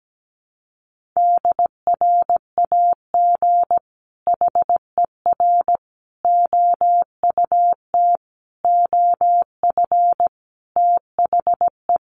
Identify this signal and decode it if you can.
Morse code, heard as a keyed tone, about 17 wpm, 700 Hz. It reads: DRAG HER OUT OF THE
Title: Sentences from Top 1500 Words 17wpm